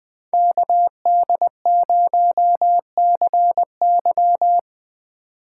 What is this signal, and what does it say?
Morse code, heard as a keyed tone, about 20 wpm, 700 Hz. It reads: KD0CY